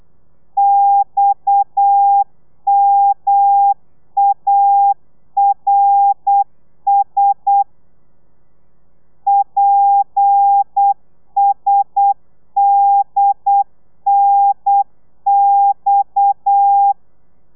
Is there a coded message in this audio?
Yes: Morse code from 0.6 to 16.9 s, heard as a keyed tone, about 8 wpm, 800 Hz. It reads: XMARS PSDNX